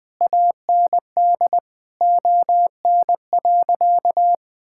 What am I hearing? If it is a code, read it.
Morse code, heard as a keyed tone, about 21 wpm, 700 Hz. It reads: AND ON.
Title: Sentences with 8-Word Phrases - 20wpm